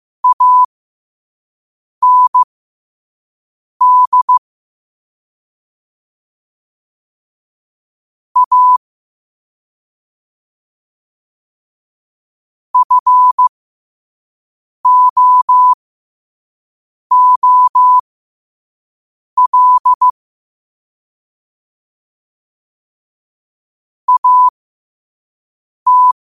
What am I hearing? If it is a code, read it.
Morse code, heard as a keyed tone, about 15 wpm, spaced as for 5 wpm, 1000 Hz. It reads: AND A FOOL AT